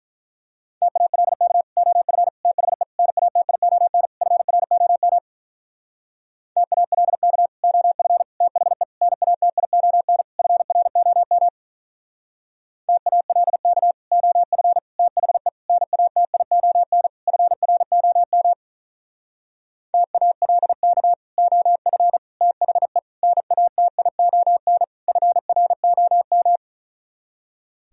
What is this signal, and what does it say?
Morse code, heard as a keyed tone, about 45 wpm, 700 Hz. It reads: TALK OF THE NATION FROM TALK OF THE NATION FROM TALK OF THE NATION FROM TALK OF THE NATION FROM